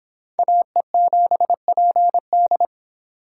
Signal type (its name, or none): Morse code